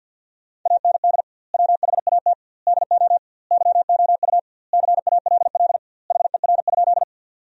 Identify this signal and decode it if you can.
Morse code, heard as a keyed tone, about 50 wpm, 700 Hz. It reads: AND WHAT DO YOU CALL HER?
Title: 2x Word-Spacing - Sentences from Top 100 Words - 50wpm